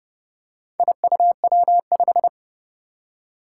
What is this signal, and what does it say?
Morse code, heard as a keyed tone, about 30 wpm, 700 Hz. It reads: IUW5
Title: Four Challenging Letter-Number Combinations 30wpm